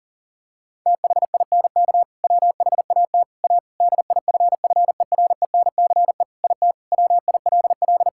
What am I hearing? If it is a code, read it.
Morse code, heard as a keyed tone, about 40 wpm, 700 Hz. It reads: THINK WHAT A DIFFERENCE IT WILL
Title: Sentences from Top 1000 Words - Review Part 5 - 40wpm